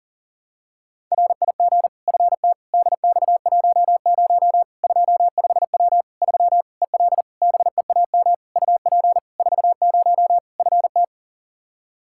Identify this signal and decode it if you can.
Morse code, heard as a keyed tone, about 37 wpm, 700 Hz. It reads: RIG FT DX10 25W 3 EL BEAM UP 40 FT
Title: Practice QSO - Exchange 2 - Info - 40wpm